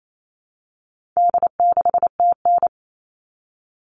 Morse code, heard as a keyed tone, about 28 wpm, 700 Hz. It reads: D6TN
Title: All Cut Numbers - 4-Digit 28wpm